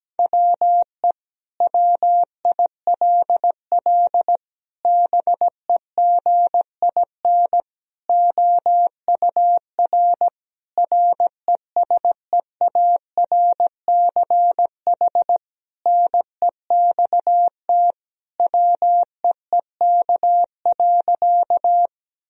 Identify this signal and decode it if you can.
Morse code, heard as a keyed tone, about 17 wpm, 700 Hz. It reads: WE WILL BEGIN OUR RESEARCH NEXT WEEK.
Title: Sentences from Top 1000 Words - Review Part 4 - 17wpm